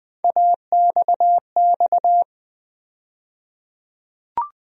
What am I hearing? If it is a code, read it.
Morse code, heard as a keyed tone, about 20 wpm, 700 Hz. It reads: AXX
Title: USA Call Sign Postfixes - 3 Letters 20wpm